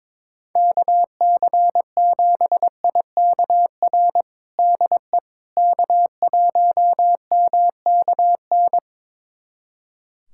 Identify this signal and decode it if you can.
Morse code, heard as a keyed tone, about 21 wpm, 700 Hz. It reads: KC7IKR DE K1MKN
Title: USA Callsign DE Callsign 22wpm